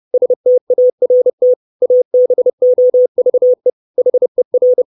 Morse code, heard as a keyed tone, about 30 wpm, 500 Hz. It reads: START ABOVE HER
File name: 632 words / 632 Words 30WPM